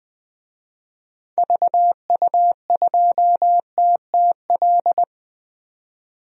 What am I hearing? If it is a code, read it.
Morse code, heard as a keyed tone, about 20 wpm, 700 Hz. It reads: VU2TTL
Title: DXCC-100 Call Signs - 6 Characters - 20wpm